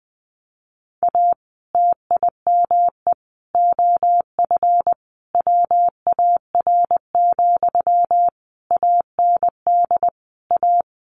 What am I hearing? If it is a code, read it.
Morse code, heard as a keyed tone, about 20 wpm, 700 Hz. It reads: A TIME OF WAR, AND A
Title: Sentences from Top 1500 Words 20wpm